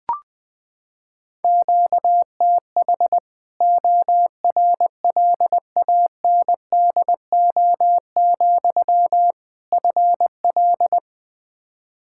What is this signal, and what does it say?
Morse code, heard as a keyed tone, about 20 wpm, 700 Hz. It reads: QTH ORLANDO, FL
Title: Practice QSO - Exchange 1 - Beginning - 20wpm